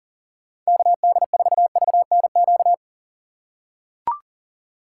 Morse code, heard as a keyed tone, about 40 wpm, 700 Hz. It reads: KD4VNQ
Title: USA Call Signs - 2x Character Spacing - 40wpm